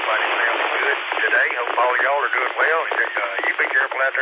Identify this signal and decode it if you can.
Emergency Alert System (SAME) tones, heard as harsh buzzing bursts, with speech. It is heard from 1.1 to 3.7 s.